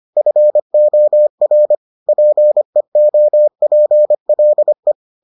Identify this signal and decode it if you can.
Morse code, heard as a keyed tone, about 25 wpm, 600 Hz. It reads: FOR PEOPLE